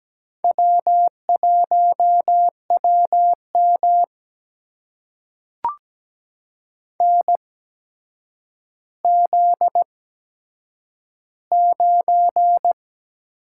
Morse code, heard as a keyed tone, about 17 wpm, spaced as for 8 wpm, 700 Hz. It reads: W1WM NZ9
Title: USA Call Signs - 4 Characters - 8x Character Spacing - 17wpm